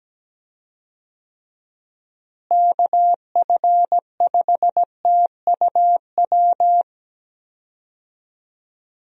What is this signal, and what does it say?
Morse code, heard as a keyed tone, about 17 wpm, 700 Hz. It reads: KF5TUW
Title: USA Call Signs - 6 Characters - Repeated 3x - 17wpm